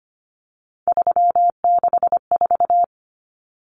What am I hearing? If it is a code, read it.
Morse code, heard as a keyed tone, about 25 wpm, 700 Hz. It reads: 364
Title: Common Cut Numbers - 3-Digit 25wpm